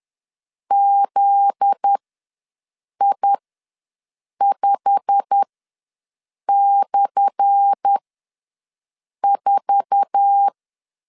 Morse code, heard as a keyed tone, about 10 wpm, spaced as for 6 wpm, 800 Hz. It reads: ZI5/4
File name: LEZIONI / 10 SIGNS / 10A SLASH ws 10wpm 4121